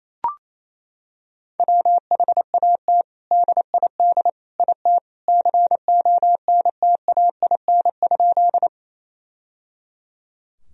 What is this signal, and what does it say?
Morse code, heard as a keyed tone, about 28 wpm, 700 Hz. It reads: WHAT DID IT CONTAIN?
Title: Sentences from Top 1000 Words - Review Part 2 - 28wpm